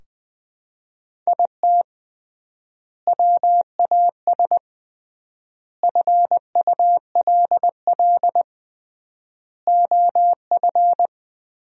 Morse code, heard as a keyed tone, about 20 wpm, 700 Hz. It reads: IT WAS FULL OF
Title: 3x Word-Spacing - Sentences from Top 500 Words - 20wpm